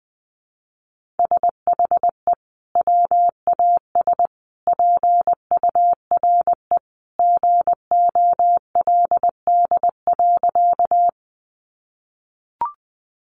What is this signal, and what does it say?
Morse code, heard as a keyed tone, about 20 wpm, 700 Hz. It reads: SHE WAS PURE GOLD.
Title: Sentences from Top 1500 Words 20wpm